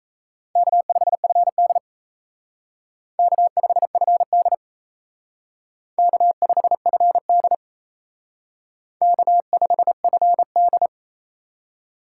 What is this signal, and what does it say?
Morse code, heard as a keyed tone, about 32 wpm, 700 Hz. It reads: K5FD K5FD K5FD K5FD
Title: Speed Racing - USA 1x2 Call Signs 28wpm